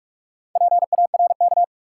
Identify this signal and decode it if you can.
Morse code, heard as a keyed tone, about 45 wpm, 700 Hz. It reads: PARK
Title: Sentences from Top 1300 Words - New Word Drill - 45wpm